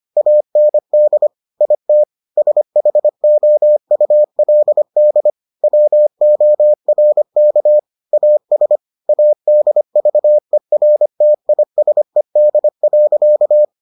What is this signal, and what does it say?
Morse code, heard as a keyed tone, about 25 wpm, 600 Hz. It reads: AND IT SHOULD WORK AS ADVERTISED.